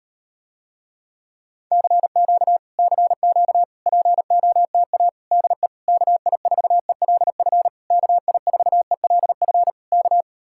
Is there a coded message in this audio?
Yes: Morse code from 1.7 to 10.2 s, heard as a keyed tone, about 38 wpm, 700 Hz. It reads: CQ CQ POTA DE KI4ELF KI4ELF K